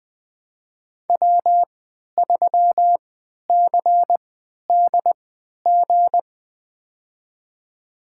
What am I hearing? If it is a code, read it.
Morse code, heard as a keyed tone, about 20 wpm, spaced as for 11 wpm, 700 Hz. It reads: W3CDG
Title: USA Call Signs - 5 Characters - 3x Character Spacing - 20wpm